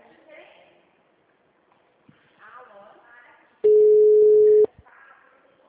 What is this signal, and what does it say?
Ringback tone (Europe, 425 Hz), with speech.